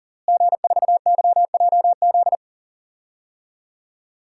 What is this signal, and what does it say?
Morse code, heard as a keyed tone, about 40 wpm, 700 Hz. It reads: G4YJZ
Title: DXCC-100 Call Signs - 5 Characters - Repeated 3x - 40wpm